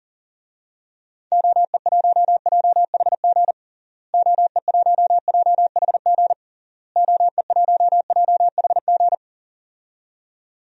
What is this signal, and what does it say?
Morse code, heard as a keyed tone, about 40 wpm, 700 Hz. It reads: OE1JHG OE1JHG OE1JHG
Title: DXCC-100 Call Signs - 6 Characters - Repeated 3x - 40wpm